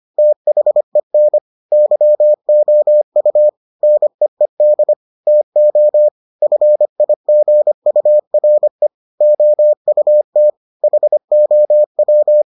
Morse code, heard as a keyed tone, about 25 wpm, 600 Hz. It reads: THEN YOU NEED TO FIGURE OUT HOW